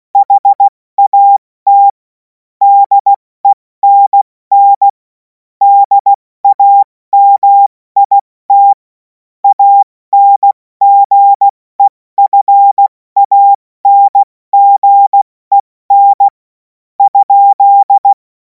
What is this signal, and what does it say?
Morse code, heard as a keyed tone, about 16 wpm, spaced as for 14 wpm, 800 Hz. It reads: HAT DENN DAMIT ANGEFANGEN?
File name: Morsekurs / Witze / Witz 21. April 2026 70 F80